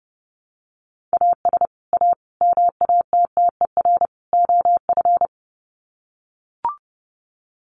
Morse code, heard as a keyed tone, about 30 wpm, 700 Hz. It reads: AS A MATTER OF
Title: Sets of 4 Words 30wpm